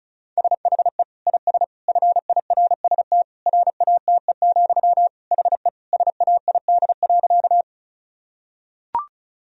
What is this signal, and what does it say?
Morse code, heard as a keyed tone, about 36 wpm, 700 Hz. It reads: SHE IS FIRST RATE, HE SAID.
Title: Sentences from Top 1000 Words - Review Part 4 - 35wpm